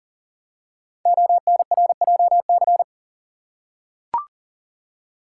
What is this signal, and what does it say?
Morse code, heard as a keyed tone, about 40 wpm, 700 Hz. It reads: ONRJC
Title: Code Groups - 5 Letters - 40wpm